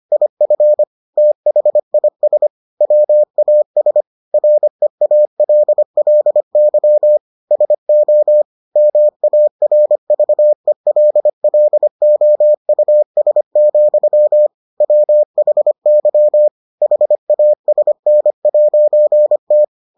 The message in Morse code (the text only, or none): IF THIS WAS REALLY SO MARVELLOUS, WHY HASN'T